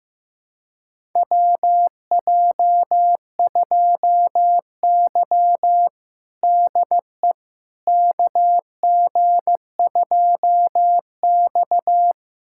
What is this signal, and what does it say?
Morse code, heard as a keyed tone, about 15 wpm, 700 Hz. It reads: WJ2Y DE KG2X
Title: USA Callsign DE Callsign - 4 Characters - 15wpm